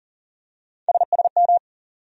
Morse code, heard as a keyed tone, about 40 wpm, 700 Hz. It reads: SSM
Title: USA Call Sign Postfixes - 3 Letters 40wpm